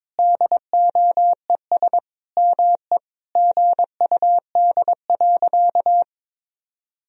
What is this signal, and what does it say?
Morse code, heard as a keyed tone, about 22 wpm, 700 Hz. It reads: DOES ME GUD.
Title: Sentences from Top 200 Words and Abbreviations 22wpm